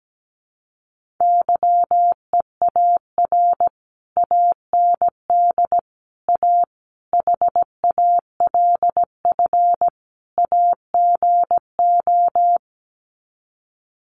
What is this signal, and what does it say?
Morse code, heard as a keyed tone, about 17 wpm, 700 Hz. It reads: YEAR AND A HALF AGO
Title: Sets of 5 Words Encore 17wpm